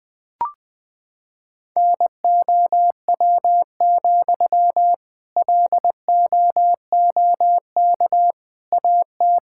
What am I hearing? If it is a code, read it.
Morse code, heard as a keyed tone, about 20 wpm, 700 Hz. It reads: NOW, LOOK AT
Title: Sentences from Top 1600 Words 20wpm